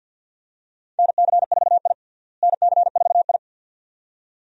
Morse code, heard as a keyed tone, about 49 wpm, 700 Hz. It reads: NC4I NC4I
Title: USA Call Signs - 4 Characters - Repeated 2x - 50wpm